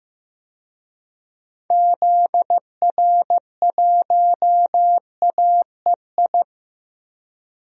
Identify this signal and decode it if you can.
Morse code, heard as a keyed tone, about 15 wpm, 700 Hz. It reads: ZR1AEI
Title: DXCC-100 Call Signs - 6 Characters - Repeated 3x - 15wpm